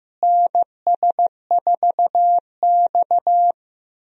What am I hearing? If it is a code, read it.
Morse code, heard as a keyed tone, about 15 wpm, 700 Hz. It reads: NS4X
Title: USA Call Signs - 4 Characters - Repeated 3x - 15wpm